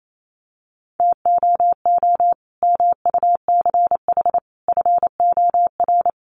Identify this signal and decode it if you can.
Morse code, heard as a keyed tone, about 28 wpm, 700 Hz. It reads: TOO MUCH FOR
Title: Sentences from Top 200 Words 28wpm